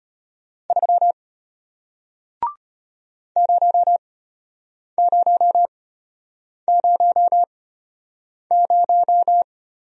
Morse code, heard as a keyed tone, about 34 wpm, spaced as for 8 wpm, 700 Hz. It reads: 3 0000